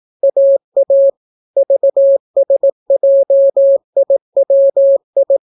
Morse code, heard as a keyed tone, about 18 wpm, 550 Hz. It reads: AA VSJIWI